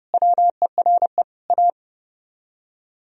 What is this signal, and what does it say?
Morse code, heard as a keyed tone, about 33 wpm, 700 Hz. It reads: WERE A